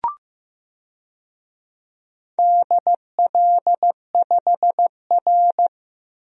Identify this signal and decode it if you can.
Morse code, heard as a keyed tone, about 15 wpm, 700 Hz. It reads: DL5R
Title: DXCC-100 Call Signs - 4 Characters - 15wpm